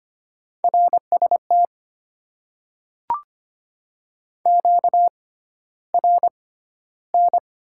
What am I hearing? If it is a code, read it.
Morse code, heard as a keyed tone, about 25 wpm, spaced as for 13 wpm, 690 Hz. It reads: RST QRN
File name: Top 90 QSO Elements - 6x Character Spacing - 25wpm